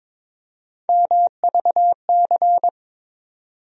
Morse code, heard as a keyed tone, about 22 wpm, 700 Hz. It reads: MVC
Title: USA Call Sign Postfixes - 3 Letters 22wpm